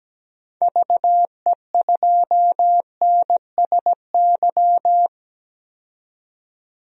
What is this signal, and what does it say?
Morse code, heard as a keyed tone, about 17 wpm, 700 Hz. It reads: VE2NSY
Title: DXCC-100 Call Signs - 6 Characters - 17wpm